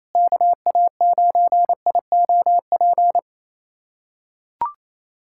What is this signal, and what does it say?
Morse code, heard as a keyed tone, about 28 wpm, 700 Hz. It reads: KA9IOP
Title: USA Callsign DE Callsign 28wpm